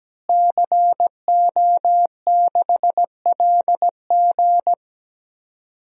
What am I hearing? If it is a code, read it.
Morse code, heard as a keyed tone, about 17 wpm, 700 Hz. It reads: CO6LG
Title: DXCC-100 Call Signs - 5 Characters - Repeated 3x - 17wpm